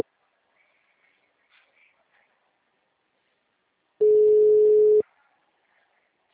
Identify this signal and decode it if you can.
Ringback tone (Europe, 425 Hz).
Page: Araras (FGTS)